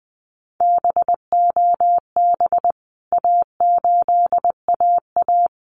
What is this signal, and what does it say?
Morse code, heard as a keyed tone, about 20 wpm, 700 Hz. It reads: BOB A8AA
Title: CWT Exchange 20wpm